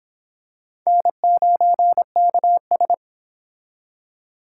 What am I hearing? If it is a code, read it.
Morse code, heard as a keyed tone, about 26 wpm, 700 Hz. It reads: N9KS